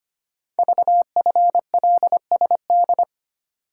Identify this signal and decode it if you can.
Morse code, heard as a keyed tone, about 25 wpm, 700 Hz. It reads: VFLSD
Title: Code Groups - 5 Letters - 25wpm